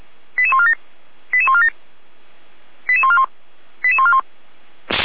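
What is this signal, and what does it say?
Five-tone selective call (ZVEI), heard as quick runs of five tones, tones of about 70 ms. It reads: 88147 88147 88141 88141
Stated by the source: Five-tone ZVEI selective call received by SDR, from the feuerwehr-orlamuende.org fire brigade audio archive.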